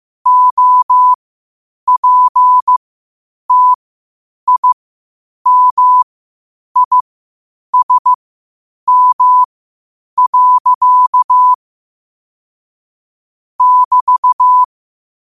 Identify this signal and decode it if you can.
Morse code, heard as a keyed tone, about 15 wpm, spaced as for 8 wpm, 1000 Hz. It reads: OPTIMISM. <BT>
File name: 8wpm